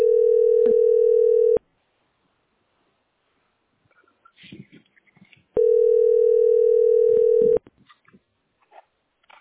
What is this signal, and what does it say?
Ringback tone (North America), 440 + 480 Hz.